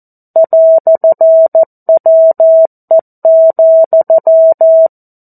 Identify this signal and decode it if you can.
Morse code, heard as a keyed tone, about 14 wpm, 650 Hz. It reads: "WE,
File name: 02-Chapter-02 14 WPM